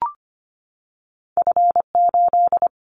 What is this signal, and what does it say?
Morse code, heard as a keyed tone, about 25 wpm, 700 Hz. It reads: F8
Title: Callsign Prefixes for Top 25 DXCC - 25wpm